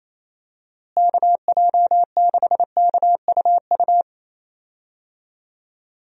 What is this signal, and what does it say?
Morse code, heard as a keyed tone, about 27 wpm, 700 Hz. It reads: KJ6KUU
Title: USA Call Signs - 6 Characters - Repeated 3x - 28wpm